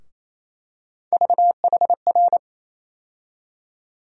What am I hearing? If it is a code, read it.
Morse code, heard as a keyed tone, about 28 wpm, 700 Hz. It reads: VHR
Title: USA Call Sign Postfixes - 3 Letters 28wpm